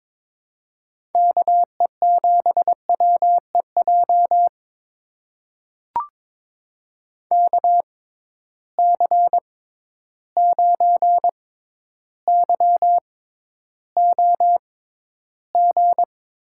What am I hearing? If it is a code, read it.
Morse code, heard as a keyed tone, about 21 wpm, spaced as for 11 wpm, 700 Hz. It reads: KE7WEJ KC9YOG
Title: USA Call Signs - 6 Characters - 6x Character Spacing - 22wpm